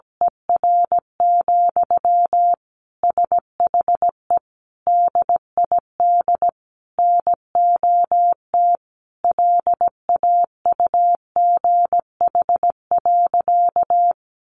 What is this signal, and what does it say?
Morse code, heard as a keyed tone, about 17 wpm, 700 Hz. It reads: ER, SHE DID NOT LAUGH.
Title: Sentences from Top 1100 Words 17wpm